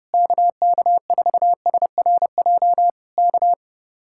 Morse code, heard as a keyed tone, about 29 wpm, 700 Hz. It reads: KK4SRJ K